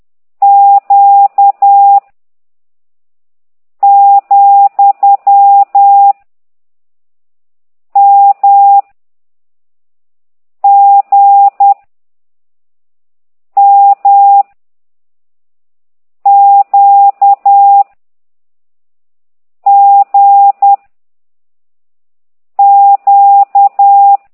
Morse code, heard as a keyed tone, about 10 wpm, 800 Hz. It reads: Q , M G M Q G Q
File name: ct2idw / exerc13